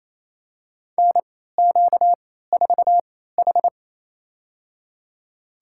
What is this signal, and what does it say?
Morse code, heard as a keyed tone, about 28 wpm, spaced as for 16 wpm, 700 Hz. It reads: NQ4H
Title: USA Call Signs - 3x Character Spacing - 28wpm